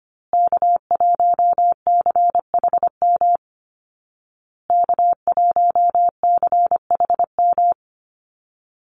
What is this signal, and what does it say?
Morse code, heard as a keyed tone, about 25 wpm, 700 Hz. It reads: K1CHM K1CHM
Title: Call Signs Repeated 2x - 25wpm